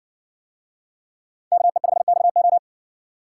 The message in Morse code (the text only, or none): KE5XO